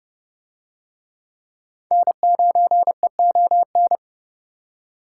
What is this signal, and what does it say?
Morse code, heard as a keyed tone, about 31 wpm, 700 Hz. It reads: N9EON